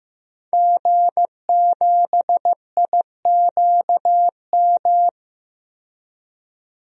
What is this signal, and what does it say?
Morse code, heard as a keyed tone, about 15 wpm, 700 Hz. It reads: G7IQM
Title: DXCC-100 Call Signs - 5 Characters - 15wpm